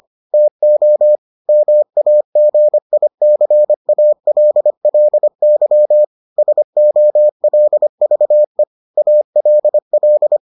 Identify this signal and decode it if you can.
Morse code, heard as a keyed tone, about 25 wpm, 600 Hz. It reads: TO MAGICALLY SOLVE ALL